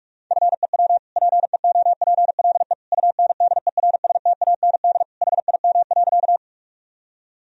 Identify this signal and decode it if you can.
Morse code, heard as a keyed tone, about 46 wpm, 700 Hz. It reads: FEW PEOPLE UNDERSTAND HIM.
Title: Sentences from Top 1000 Words - Review Part 1 - 45wpm